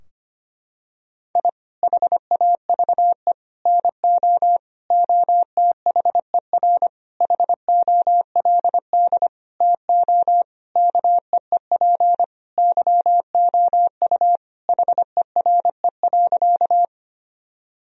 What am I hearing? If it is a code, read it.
Morse code, heard as a keyed tone, about 25 wpm, 700 Hz. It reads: I HAVE NO OTHER HOLD TO KEEP YOU HERE.
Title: Sentences from Top 1000 Words - Review Part 2 - 25wpm